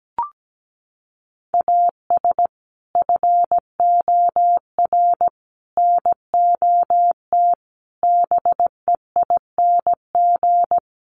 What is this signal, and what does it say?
Morse code, heard as a keyed tone, about 17 wpm, 700 Hz. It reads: AS FOR NOT BEING
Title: Sentences from Top 1100 Words 17wpm